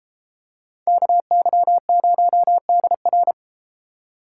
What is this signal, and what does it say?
Morse code, heard as a keyed tone, about 33 wpm, 700 Hz. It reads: KY0DR